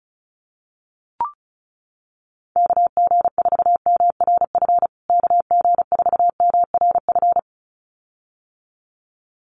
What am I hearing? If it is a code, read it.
Morse code, heard as a keyed tone, about 34 wpm, 700 Hz. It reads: KG4MRF KG4MRF
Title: USA Call Signs Repeated 2x 35wpm